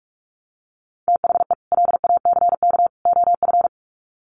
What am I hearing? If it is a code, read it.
Morse code, heard as a keyed tone, about 44 wpm, 700 Hz. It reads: THE LACK OF
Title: Sets of 3 Words 45wpm